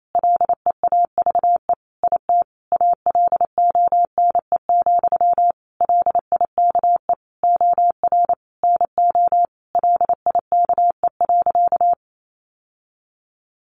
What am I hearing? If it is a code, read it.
Morse code, heard as a keyed tone, about 28 wpm, 700 Hz. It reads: LEAVE IT ALONE, LIKE OR NO LIKE.
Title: Sentences from Top 1000 Words 28wpm